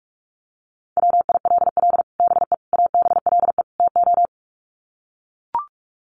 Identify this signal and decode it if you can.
Morse code, heard as a keyed tone, about 44 wpm, 700 Hz. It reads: WILL BE ABLE TO
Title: Sets of 4 Words 45wpm